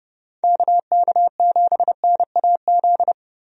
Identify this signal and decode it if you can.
Morse code, heard as a keyed tone, about 30 wpm, 700 Hz. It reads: KK7NAZ
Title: USA Call Signs - 6 Characters - Repeated 3x - 30wpm